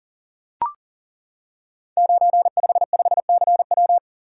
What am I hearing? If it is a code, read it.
Morse code, heard as a keyed tone, about 42 wpm, 690 Hz. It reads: 955CW
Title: Code Groups - 5 Letter:Numbers - 40wpm